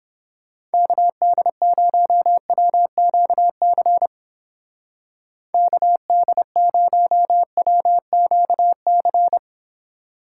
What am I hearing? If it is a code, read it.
Morse code, heard as a keyed tone, about 29 wpm, 700 Hz. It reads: KD0WQC KD0WQC